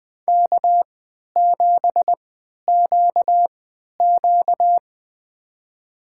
Morse code, heard as a keyed tone, about 20 wpm, spaced as for 11 wpm, 700 Hz. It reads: K7QQ
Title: USA Call Signs - 3x Character Spacing - 20wpm